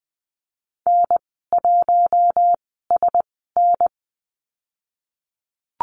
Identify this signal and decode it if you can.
Morse code, heard as a keyed tone, about 20 wpm, spaced as for 14 wpm, 700 Hz. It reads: N1SN
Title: 062 Lesson 11g - Letter h - Call Signs 20wpm